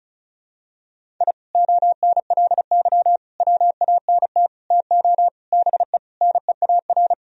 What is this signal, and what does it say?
Morse code, heard as a keyed tone, about 36 wpm, 700 Hz. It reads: I ONLY WANT TO BE NEAR
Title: Sentences from Top 1000 Words - Review Part 1 - 35wpm